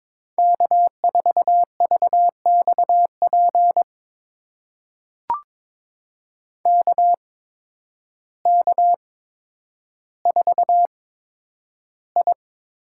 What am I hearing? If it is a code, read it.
Morse code, heard as a keyed tone, about 22 wpm, spaced as for 10 wpm, 700 Hz. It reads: K4VXP KK4I
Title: USA Call Signs - 8x Character Spacing - 22wpm